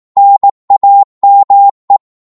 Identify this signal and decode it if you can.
Morse code, heard as a keyed tone, about 18 wpm, 800 Hz. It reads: NAME
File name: Morse / qso-18wpm